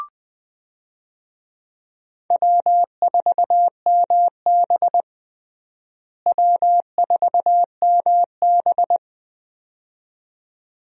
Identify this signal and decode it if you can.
Morse code, heard as a keyed tone, about 20 wpm, 700 Hz. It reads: W4MB W4MB
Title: USA Call Signs - 4 Characters - Repeated 2x - 20wpm